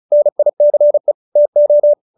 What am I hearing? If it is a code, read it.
Morse code, heard as a keyed tone, about 34 wpm, 590 Hz. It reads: NICE TO